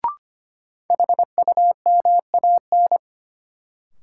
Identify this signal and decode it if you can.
Morse code, heard as a keyed tone, about 25 wpm, 700 Hz. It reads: HUMAN
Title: Sentences from Top 600 Words - New Word Drill - 25wpm